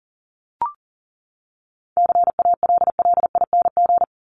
Morse code, heard as a keyed tone, about 40 wpm, 700 Hz. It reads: CALLING
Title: ING Suffix - Top 100 Words with Base Words 40wpm